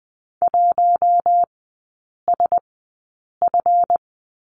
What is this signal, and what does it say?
Morse code, heard as a keyed tone, about 20 wpm, spaced as for 8 wpm, 700 Hz. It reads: 1SF
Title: 138 Lesson 23e - Letter f - Three Characters 20wpm